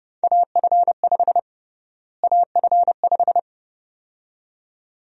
Morse code, heard as a keyed tone, about 30 wpm, 700 Hz. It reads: AF5 AF5
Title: USA Call Sign Prefixes - Repeated 2x - 30wpm